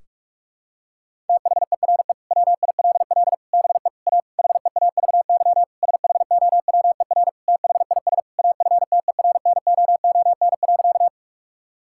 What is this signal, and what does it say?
Morse code, heard as a keyed tone, about 46 wpm, 700 Hz. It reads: THERE WILL BE A HEAVY SHOWER THIS AFTERNOON.
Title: Sentences from Top 1600 Words 45wpm